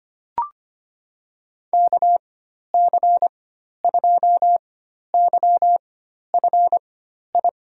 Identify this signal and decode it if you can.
Morse code, heard as a keyed tone, about 25 wpm, spaced as for 12 wpm, 700 Hz. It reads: KC2YFI
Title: USA Call Signs - 4x Character Spacing - 25wpm